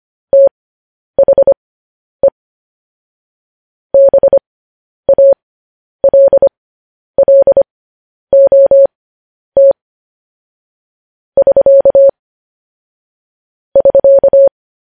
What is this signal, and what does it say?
Morse code, heard as a keyed tone, about 25 wpm, spaced as for 10 wpm, 550 Hz. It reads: THE BALLOT <SK> <SK>
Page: news headlines in morse code at 10 wpm